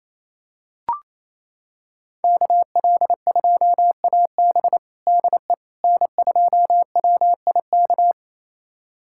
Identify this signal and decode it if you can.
Morse code, heard as a keyed tone, about 28 wpm, 700 Hz. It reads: KL2AB DE N2WIK
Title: USA Callsign DE Callsign - 5 Characters - 28wpm